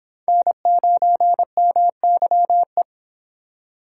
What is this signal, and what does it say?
Morse code, heard as a keyed tone, about 25 wpm, 700 Hz. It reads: N9MYE